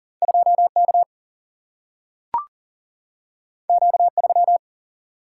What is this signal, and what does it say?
Morse code, heard as a keyed tone, about 39 wpm, 700 Hz. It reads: 2K Q3
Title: Single-Letter Single-Number 40wpm